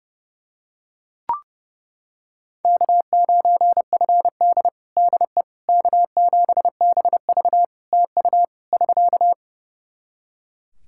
Morse code, heard as a keyed tone, about 32 wpm, 700 Hz. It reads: K9FD DE K7BV TU <SK>
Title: Practice QSO - Exchange 4 - Ending - 30wpm